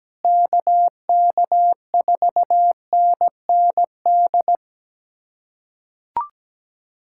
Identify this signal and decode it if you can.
Morse code, heard as a keyed tone, about 17 wpm, 700 Hz. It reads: KK4NND
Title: USA Callsign DE Callsign 17wpm